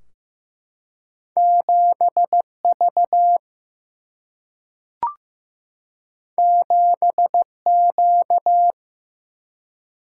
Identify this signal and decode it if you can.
Morse code, heard as a keyed tone, about 15 wpm, 700 Hz. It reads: 7V 7Q
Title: Single-Letter Single-Number 15wpm